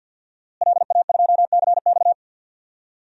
Morse code, heard as a keyed tone, about 52 wpm, 700 Hz. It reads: LA2CX
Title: DXCC-100 Call Signs - 5 Characters - Repeated 2x - 50wpm